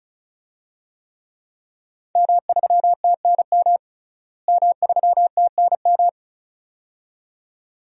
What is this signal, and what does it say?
Morse code, heard as a keyed tone, about 36 wpm, 700 Hz. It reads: M3TNM M3TNM
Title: DXCC-100 Call Signs - 5 Characters - Repeated 2x - 35wpm